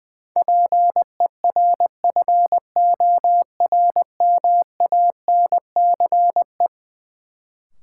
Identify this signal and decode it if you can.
Morse code, heard as a keyed tone, about 20 wpm, 700 Hz. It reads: PERFORMANCE
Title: Sentences from Top 1000 Words - New Word Drill - 20wpm